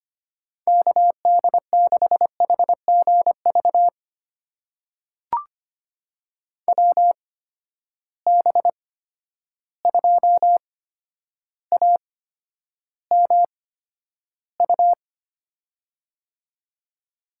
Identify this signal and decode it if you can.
Morse code, heard as a keyed tone, about 25 wpm, spaced as for 11 wpm, 700 Hz. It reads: KD6HGV WB2AMU
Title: USA Call Signs - 6 Characters - 8x Character Spacing - 25wpm